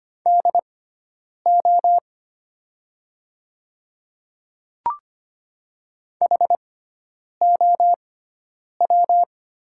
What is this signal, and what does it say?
Morse code, heard as a keyed tone, about 25 wpm, spaced as for 9 wpm, 700 Hz. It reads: DO HOW
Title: 6x Character Spacing - Top 200 Words - 25wpm